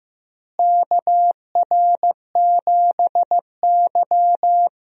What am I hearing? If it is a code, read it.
Morse code, heard as a keyed tone, about 15 wpm, 700 Hz. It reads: KR7Y
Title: USA Call Signs - 4 Characters - Repeated 3x - 15wpm